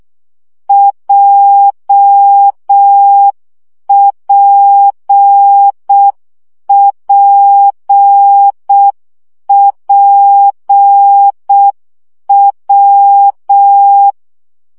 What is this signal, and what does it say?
Morse code, heard as a keyed tone, about 6 wpm, 800 Hz. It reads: JPPPW
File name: CW gyakorlas w p j 5 baud